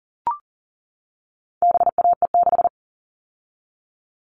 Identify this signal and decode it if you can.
Morse code, heard as a keyed tone, about 40 wpm, 700 Hz. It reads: BAE6
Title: All Cut Numbers - 4-Digit 40wpm